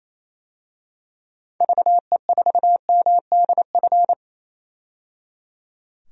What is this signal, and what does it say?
Morse code, heard as a keyed tone, about 28 wpm, 700 Hz. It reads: VE4MDF
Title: DXCC-100 Call Signs - 6 Characters - 28wpm